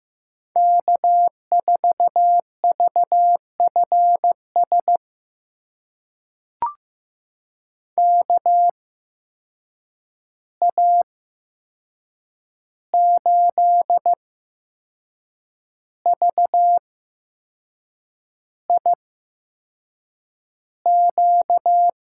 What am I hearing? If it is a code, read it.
Morse code, heard as a keyed tone, about 15 wpm, spaced as for 6 wpm, 700 Hz. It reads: K4VFS KA8VIQ
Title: USA Call Signs - 8x Character Spacing - 15wpm